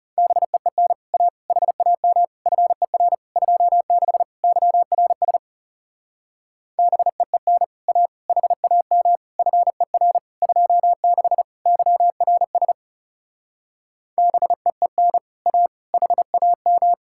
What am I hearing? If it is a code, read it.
Morse code, heard as a keyed tone, about 38 wpm, 700 Hz. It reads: BEEN A HAM FER 26 YRS BEEN A HAM FER 26 YRS BEEN A HAM